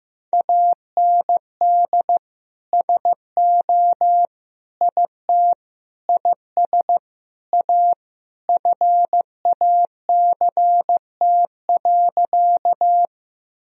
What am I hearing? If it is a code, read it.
Morse code, heard as a keyed tone, about 15 wpm, 700 Hz. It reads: AND SO IT IS A FACT.
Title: Sentences from Top 1000 Words - Review Part 3 - 15wpm